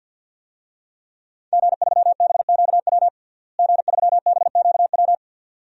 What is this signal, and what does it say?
Morse code, heard as a keyed tone, about 46 wpm, 700 Hz. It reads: G3BQW G3BQW
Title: DXCC-100 Call Signs - 5 Characters - Repeated 2x - 50wpm